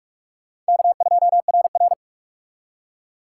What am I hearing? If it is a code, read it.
Morse code, heard as a keyed tone, about 63 wpm, spaced as for 54 wpm, 700 Hz. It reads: K2RR
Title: Speed Racing - USA 1x2 Call Signs 45wpm